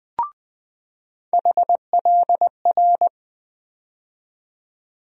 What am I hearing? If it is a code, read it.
Morse code, heard as a keyed tone, about 20 wpm, 700 Hz. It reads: HLR
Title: USA Call Sign Postfixes - 3 Letters 20wpm